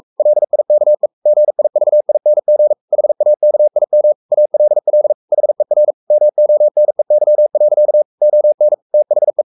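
Morse code, heard as a keyed tone, about 44 wpm, 600 Hz. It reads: LIKE GIVING HAKIM ALL HER MONEY. ON THE
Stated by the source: Morse code at 43 wpm sending Ken Follett's novel Never in English, from CW med Gnister.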